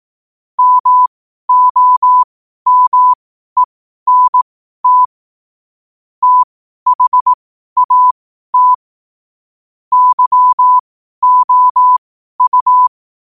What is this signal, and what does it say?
Morse code, heard as a keyed tone, about 18 wpm, spaced as for 12 wpm, 1000 Hz. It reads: MOMENT THAT YOU
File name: qotdfiles / 12wpm